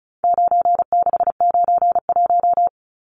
Morse code, heard as a keyed tone, about 34 wpm, 700 Hz. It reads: 9691
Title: Numbers - 4-Digit 35wpm